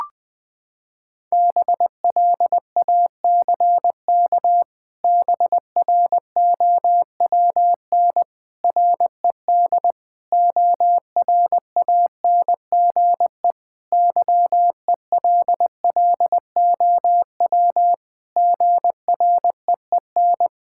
Morse code, heard as a keyed tone, about 20 wpm, 700 Hz. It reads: BLACK BROWN RED ORANGE YELLOW GREEN
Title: Pangram Challenge v2 - 20wpm